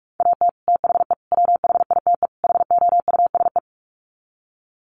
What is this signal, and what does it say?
Morse code, heard as a keyed tone, about 44 wpm, 700 Hz. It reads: AT THE WHITE HOUSE
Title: Sets of 4 Words 45wpm